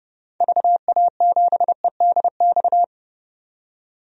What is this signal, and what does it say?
Morse code, heard as a keyed tone, about 30 wpm, 700 Hz. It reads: VA7EDX
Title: DXCC-100 Call Signs - 6 Characters - 30wpm